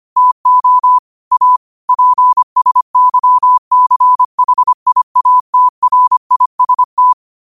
Morse code, heard as a keyed tone, about 25 wpm, 1000 Hz. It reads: TO A PSYCHIATRIST
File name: qotdfiles / 25wpm